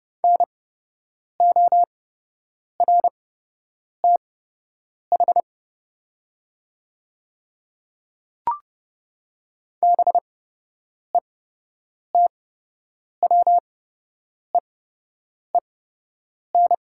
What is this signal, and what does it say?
Morse code, heard as a keyed tone, about 31 wpm, spaced as for 8 wpm, 700 Hz. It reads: NORTH BETWEEN
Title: 8x Character Spacing - Top 500 Words - 30wpm